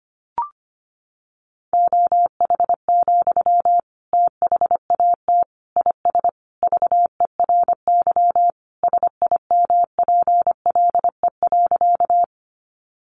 Morse code, heard as a keyed tone, about 25 wpm, 700 Hz. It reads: OH, THAT IS VERY SIMPLE.
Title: Sentences from Top 500 Words 25wpm